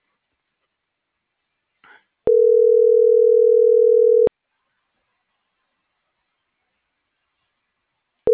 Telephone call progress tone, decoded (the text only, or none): ringback tone